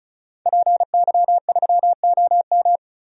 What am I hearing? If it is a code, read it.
Morse code, heard as a keyed tone, about 34 wpm, 700 Hz. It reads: PY3OM
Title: DXCC-100 Call Signs - 5 Characters - 35wpm